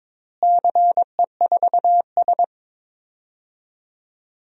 Morse code, heard as a keyed tone, about 22 wpm, 700 Hz. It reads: CE4S
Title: DXCC-100 Call Signs - 4 Characters - Repeated 2x - 22wpm